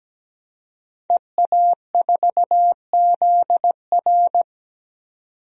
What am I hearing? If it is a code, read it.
Morse code, heard as a keyed tone, about 17 wpm, 700 Hz. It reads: EA4ZR
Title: DXCC-100 Call Signs - 5 Characters - 17wpm